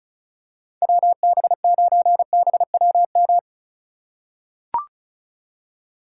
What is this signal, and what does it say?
Morse code, heard as a keyed tone, about 36 wpm, 700 Hz. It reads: WB9BWM
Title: USA Call Signs - 6 Characters - 4x Character Spacing - 35wpm